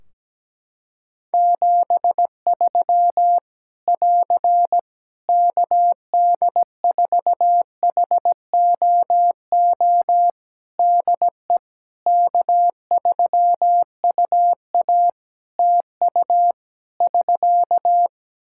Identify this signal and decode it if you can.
Morse code, heard as a keyed tone, about 17 wpm, 700 Hz. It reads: 73 <AR> KD4HOO DE K3UA TU <SK>
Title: Practice QSO - Exchange 4 - Ending - 17wpm